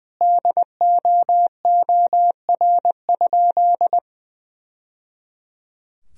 Morse code, heard as a keyed tone, about 21 wpm, 700 Hz. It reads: DOOR?
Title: Sentences from Top 1000 Words - Review Part 2 - 20wpm